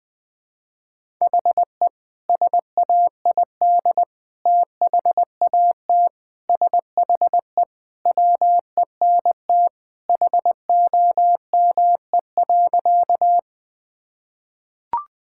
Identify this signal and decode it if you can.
Morse code, heard as a keyed tone, about 20 wpm, 700 Hz. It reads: HE SAID THAT SHE WENT HOME.
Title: Sentences with 2-Word Phrases - 20wpm